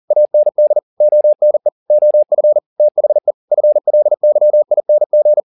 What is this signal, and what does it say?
Morse code, heard as a keyed tone, about 40 wpm, 600 Hz. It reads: AND ONE OF THE FLYING
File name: the old man and the sea 2 40wpm